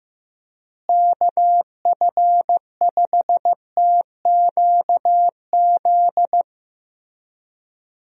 Morse code, heard as a keyed tone, about 15 wpm, 700 Hz. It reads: KF5TQZ
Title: USA Call Signs - 6 Characters - Repeated 2x - 15wpm